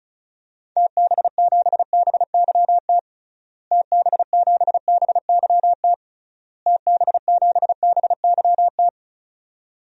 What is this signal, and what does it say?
Morse code, heard as a keyed tone, about 36 wpm, 700 Hz. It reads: TB7BYT TB7BYT TB7BYT
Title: DXCC-100 Call Signs - 6 Characters - Repeated 3x - 35wpm